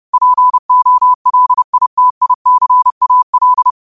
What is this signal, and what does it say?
Morse code, heard as a keyed tone, about 30 wpm, 1000 Hz. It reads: POLITICAL
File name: qotdfiles / 30wpm